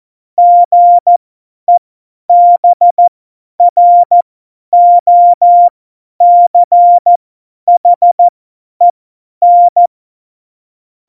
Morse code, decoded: GEBROCHEN

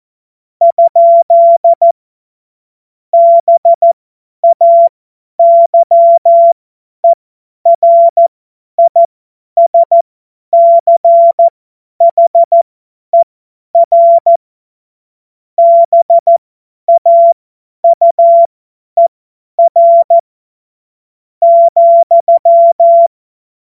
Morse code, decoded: ? BAYERISCHER BAUER,

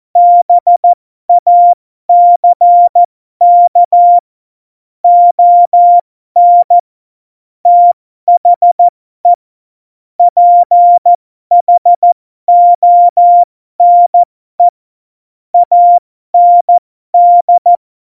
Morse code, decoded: BACK ON THE PHONE AND

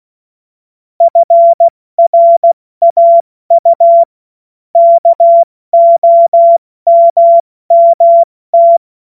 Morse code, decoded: FRAU KOMMT

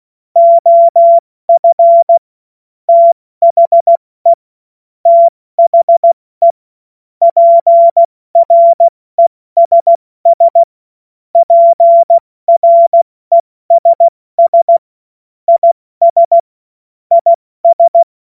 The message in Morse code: OF THE THE PRESS PRESS IS IS

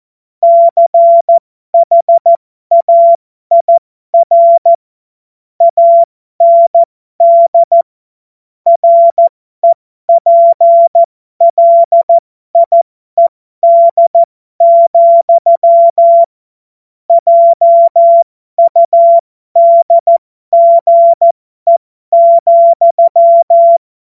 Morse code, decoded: CHAIR AND REPLIED, JUDGE,

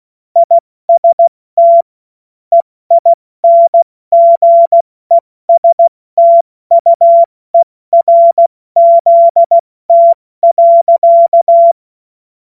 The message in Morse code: IST EINGESTUERZT.

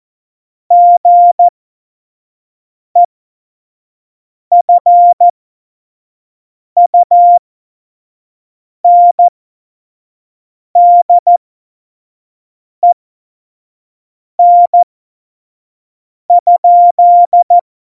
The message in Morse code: GEFUNDEN?

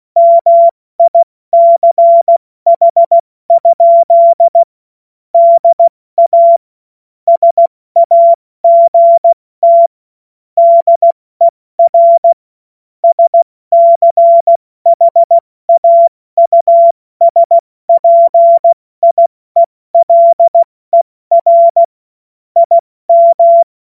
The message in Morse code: MICH? DA SAGT DER SCHAUSPIELER IM

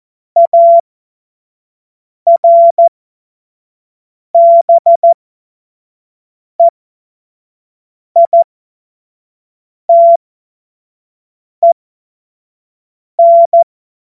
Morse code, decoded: ARBEITEN

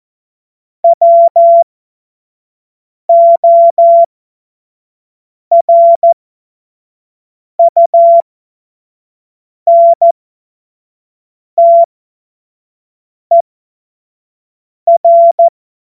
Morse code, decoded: WORUNTER

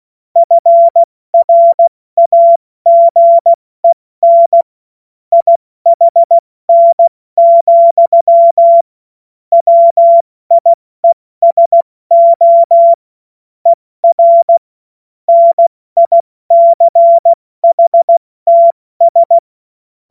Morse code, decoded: FRAGEN IHN, WIESO ER NICHTS

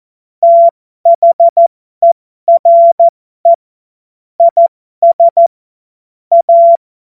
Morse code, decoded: THERE IS A

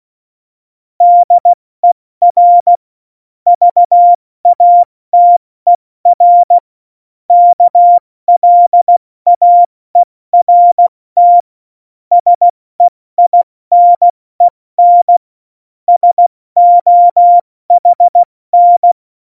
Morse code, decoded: DER VATER KLAERT SEINEN SOHN

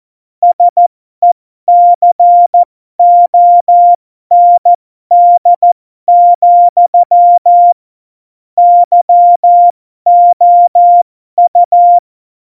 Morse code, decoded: SECOND, YOU

